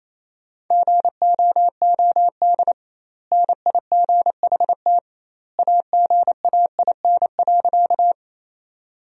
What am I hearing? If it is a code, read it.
Morse code, heard as a keyed tone, about 28 wpm, 700 Hz. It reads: GOOD NIGHT AGAIN.
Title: 2x Word-Spacing - Sentences from Top 300 Words - 28wpm